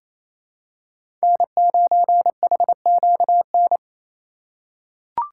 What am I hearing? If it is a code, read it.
Morse code, heard as a keyed tone, about 28 wpm, 700 Hz. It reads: N9HQN
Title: USA Call Signs - 2x Character Spacing - 28wpm